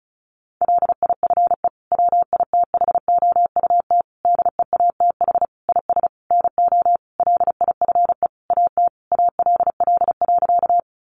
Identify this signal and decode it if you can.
Morse code, heard as a keyed tone, about 34 wpm, 700 Hz. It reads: LIFE WITHOUT DEATH IS NO LIFE AT ALL.
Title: Sentences from Top 1100 Words 35wpm